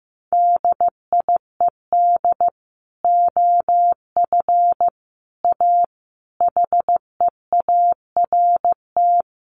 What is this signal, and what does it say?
Morse code, heard as a keyed tone, about 15 wpm, 700 Hz. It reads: DIED OF A HEART
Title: Sets of 5 Words Encore2 15wpm